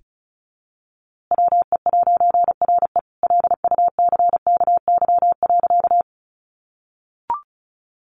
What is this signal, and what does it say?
Morse code, heard as a keyed tone, about 34 wpm, 700 Hz. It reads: WE'RE LUCKY.
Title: A Sound Of Thunder 35wpm